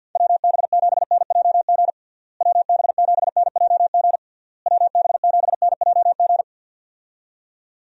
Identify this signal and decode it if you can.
Morse code, heard as a keyed tone, about 48 wpm, 700 Hz. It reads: WB7NJG WB7NJG WB7NJG
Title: USA Call Signs - 6 Characters - Repeated 3x - 50wpm